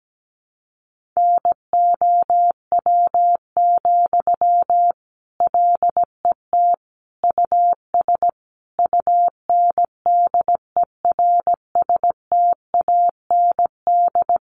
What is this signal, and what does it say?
Morse code, heard as a keyed tone, about 17 wpm, 700 Hz. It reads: NOW, LET US UNDERSTAND
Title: Sentences from Top 1500 Words 17wpm